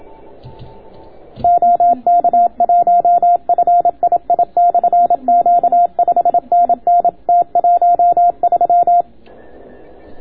Morse code, heard as a keyed tone, about 27 wpm, 690 Hz. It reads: OK1FII/Q 5NN T13